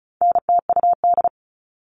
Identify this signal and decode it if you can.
Morse code, heard as a keyed tone, about 34 wpm, 700 Hz. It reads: NTUD
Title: All Cut Numbers - 4-Digit 35wpm